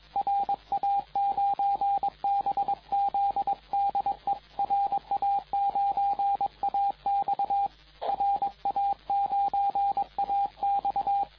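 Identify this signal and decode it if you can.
Morse code, heard as a keyed tone, about 21 wpm, 800 Hz, with radio noise. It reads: RA9BZ DE RA9AX RA9AX
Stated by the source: CW contest exchange in the 2014 Popov Cup VHF contest, RA9AX calling RA9BZ, recorded by the TR4W logger.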